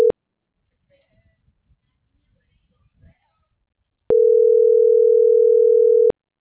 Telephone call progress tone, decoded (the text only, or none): ringback tone